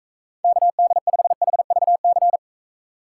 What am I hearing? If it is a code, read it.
Morse code, heard as a keyed tone, about 44 wpm, 700 Hz. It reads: KD5HVC